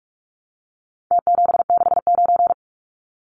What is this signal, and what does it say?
Morse code, heard as a keyed tone, about 44 wpm, 700 Hz. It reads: T769
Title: Numbers - Contest Style - 4-Digit 45wpm